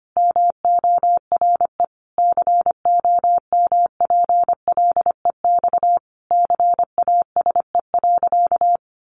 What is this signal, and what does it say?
Morse code, heard as a keyed tone, about 25 wpm, 700 Hz. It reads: MORE COMPLEX CASE.
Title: Sentences from Top 900 Words 25wpm